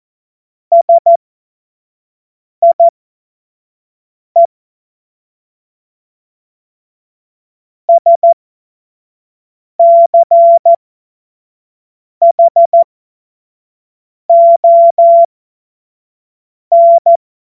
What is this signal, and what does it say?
Morse code, heard as a keyed tone, about 14 wpm, spaced as for 5 wpm, 680 Hz. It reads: SIE SCHON
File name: Morsekurs / Witze / Witz 25. Juli 2023 25 F70